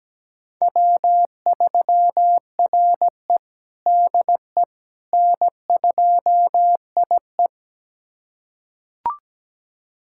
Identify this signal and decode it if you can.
Morse code, heard as a keyed tone, about 17 wpm, 700 Hz. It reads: W3RE DE N2IE
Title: USA Callsign DE Callsign - 4 Characters - 17wpm